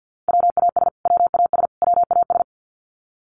Morse code, heard as a keyed tone, about 49 wpm, 700 Hz. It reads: WAS WAS WAS
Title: Top 500 Words Repeated 50wpm